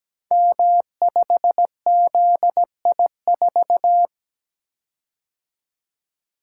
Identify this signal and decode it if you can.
Morse code, heard as a keyed tone, about 17 wpm, 700 Hz. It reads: M5ZI4
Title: Code Groups - 5 Letter:Numbers - 17wpm